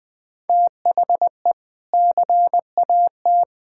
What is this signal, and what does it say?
Morse code, heard as a keyed tone, about 20 wpm, 700 Hz. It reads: THE CAT
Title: Sentences with 2-Word Phrases - 20wpm